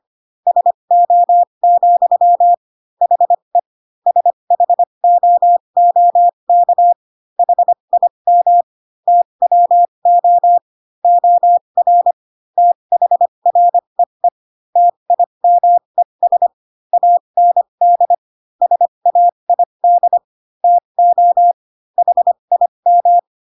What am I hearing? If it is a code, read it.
Morse code, heard as a keyed tone, about 25 wpm, spaced as for 22 wpm, 700 Hz. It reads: SO, HE SHOOK HIM TWO OR THREE TIMES AND SAID TO HIM
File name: Pinocchio / Pinocchio ch3 18 25 ik1hge